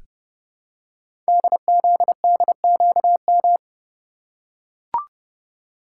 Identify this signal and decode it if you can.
Morse code, heard as a keyed tone, about 30 wpm, 700 Hz. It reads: DZDQM
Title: Code Groups - 5 Letters - 30wpm